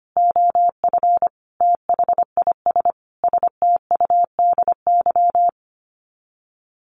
Morse code, heard as a keyed tone, about 25 wpm, 700 Hz. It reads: OF THIS STUDY
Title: Sets of 5 Words 25wpm